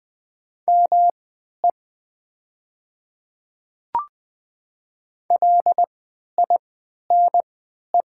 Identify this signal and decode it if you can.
Morse code, heard as a keyed tone, about 20 wpm, spaced as for 11 wpm, 700 Hz. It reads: ME LINE
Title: 3x Character Spacing - Top 200 Words - 20wpm